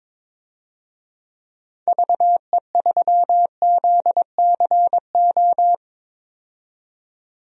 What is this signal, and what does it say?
Morse code, heard as a keyed tone, about 22 wpm, 700 Hz. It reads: VE3ZCO
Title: DXCC-100 Call Signs - 6 Characters - 22wpm